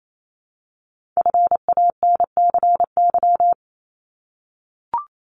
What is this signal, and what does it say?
Morse code, heard as a keyed tone, about 28 wpm, 700 Hz. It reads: FANCY
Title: Multi-letter Phonemes 28wpm